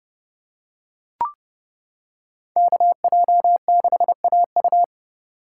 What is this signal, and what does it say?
Morse code, heard as a keyed tone, about 28 wpm, 700 Hz. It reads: KJ6AU